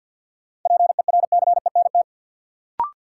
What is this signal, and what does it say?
Morse code, heard as a keyed tone, about 46 wpm, 700 Hz. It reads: PERCENT
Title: Sets of 2 Words - New Word Drill - 50wpm